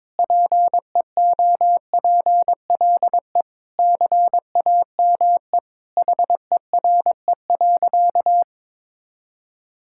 Morse code, heard as a keyed tone, about 21 wpm, 700 Hz. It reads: PEOPLE CAME HERE.
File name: practice / 22wpm / Sentences with 2-Word Phrases - 22wpm